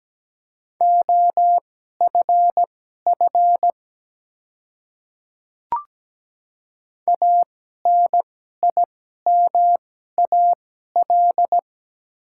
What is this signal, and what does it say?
Morse code, heard as a keyed tone, about 17 wpm, spaced as for 12 wpm, 700 Hz. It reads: OFF ANIMAL
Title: 2x Character Spacing - Top 200 Words - 17wpm